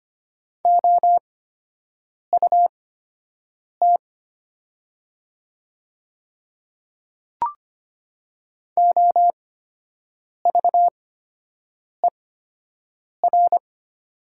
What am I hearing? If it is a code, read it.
Morse code, heard as a keyed tone, about 25 wpm, spaced as for 7 wpm, 700 Hz. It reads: OUT OVER
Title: 8x Character Spacing - Top 100 Words - 25wpm